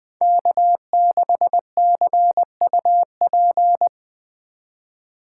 Morse code, heard as a keyed tone, about 23 wpm, 700 Hz. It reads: K6CUP
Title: Speed Racing - USA 1x3 Call Signs 20wpm